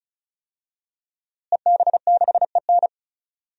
Morse code, heard as a keyed tone, about 36 wpm, 700 Hz. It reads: EB6EN
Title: DXCC-100 Call Signs - 5 Characters - Repeated 2x - 35wpm